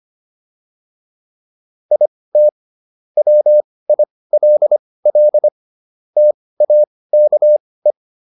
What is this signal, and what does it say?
Morse code, heard as a keyed tone, about 25 wpm, spaced as for 18 wpm, 600 Hz. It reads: IT WILL TAKE